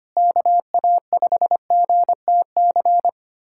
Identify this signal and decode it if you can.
Morse code, heard as a keyed tone, about 25 wpm, 700 Hz. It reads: KA5GTC
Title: USA Callsign DE Callsign - 6 Characters - 25wpm